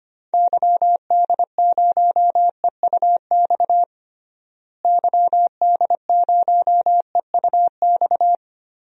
Morse code, heard as a keyed tone, about 25 wpm, 690 Hz. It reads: YD0EUX YD0EUX
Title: DXCC-100 Call Signs - 6 Characters - Repeated 3x - 25wpm